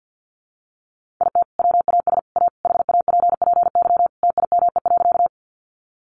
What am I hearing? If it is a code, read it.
Morse code, heard as a keyed tone, about 50 wpm, 700 Hz. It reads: IT WAS A HAPPY TIME.
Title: Sentences from Top 1000 Words 50wpm